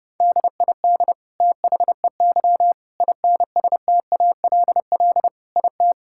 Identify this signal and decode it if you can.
Morse code, heard as a keyed tone, about 32 wpm, 700 Hz. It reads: DID THEY INSTALL IT
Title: Sentences from Top 1600 Words 30wpm